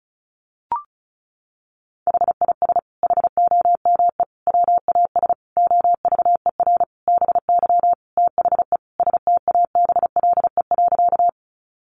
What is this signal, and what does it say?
Morse code, heard as a keyed tone, about 34 wpm, 700 Hz. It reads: HIS HOME WAS OVER BY THE STABLE.
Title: Sentences from Top 1500 Words 35wpm